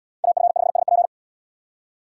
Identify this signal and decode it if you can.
Morse code, heard as a keyed tone, about 70 wpm, 700 Hz. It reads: AF5SP